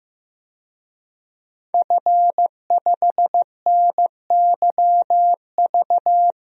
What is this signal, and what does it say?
Morse code, heard as a keyed tone, about 15 wpm, 700 Hz. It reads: F5NYV
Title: DXCC-100 Call Signs - 5 Characters - Repeated 3x - 15wpm